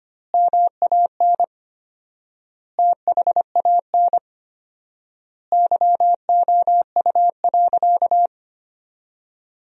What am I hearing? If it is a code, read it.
Morse code, heard as a keyed tone, about 25 wpm, 700 Hz. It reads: MAN THAN YOU.
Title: 4x Word-Spacing - Sentences from Top 400 Words - 25wpm